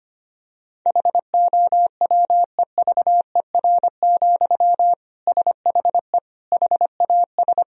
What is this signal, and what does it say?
Morse code, heard as a keyed tone, about 25 wpm, 700 Hz. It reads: HOWEVER, SHE HAS
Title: Sentences from Top 1000 Words - Review Part 2 - 25wpm